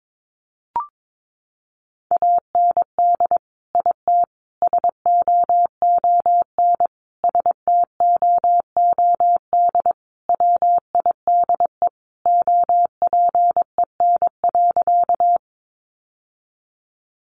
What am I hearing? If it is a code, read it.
Morse code, heard as a keyed tone, about 22 wpm, 700 Hz. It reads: AND IT SOON STOOD WIDE OPEN.
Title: Sentences from Top 900 Words 22wpm